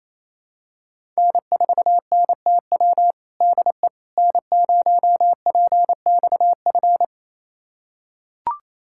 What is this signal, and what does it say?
Morse code, heard as a keyed tone, about 28 wpm, 700 Hz. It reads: N4NTW DE N0PXF
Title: USA Callsign DE Callsign - 5 Characters - 28wpm